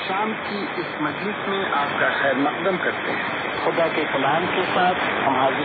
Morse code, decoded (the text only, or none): none